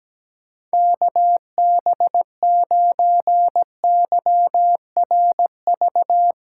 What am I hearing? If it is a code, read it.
Morse code, heard as a keyed tone, about 17 wpm, 700 Hz. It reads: KB9YRV
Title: USA Call Signs - 6 Characters - 4x Character Spacing - 17wpm